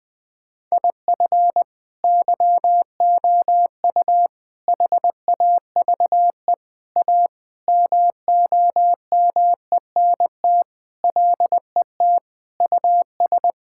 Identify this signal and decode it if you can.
Morse code, heard as a keyed tone, about 20 wpm, 700 Hz. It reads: IF YOU HAVE A MOMENT LET US
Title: Sentences with 4-Word Phrases - Spoken Beginning - 20wpm